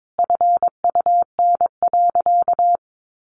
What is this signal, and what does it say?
Morse code, heard as a keyed tone, about 22 wpm, 700 Hz. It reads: FUN.
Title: Easy Phrases 22wpm